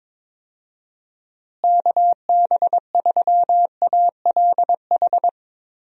Morse code, heard as a keyed tone, about 22 wpm, 700 Hz. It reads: KB3ALH
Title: USA Call Signs - 6 Characters - Repeated 3x - 22wpm